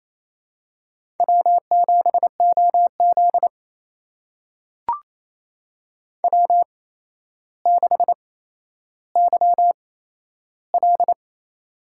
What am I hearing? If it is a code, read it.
Morse code, heard as a keyed tone, about 28 wpm, spaced as for 12 wpm, 700 Hz. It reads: W7OZ W6YL
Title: USA Call Signs - 4 Characters - 8x Character Spacing - 28wpm